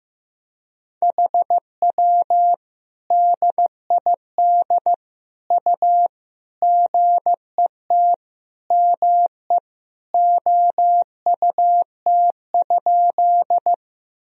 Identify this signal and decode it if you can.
Morse code, heard as a keyed tone, about 15 wpm, 700 Hz. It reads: HW DID U GET ME OUT?
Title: Sentences from Top 200 Words and Abbreviations 15wpm